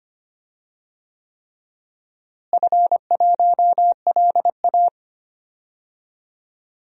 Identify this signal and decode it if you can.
Morse code, heard as a keyed tone, about 25 wpm, 700 Hz. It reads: F1LA
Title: DXCC-100 Call Signs - 4 Characters - Repeated 3x - 25wpm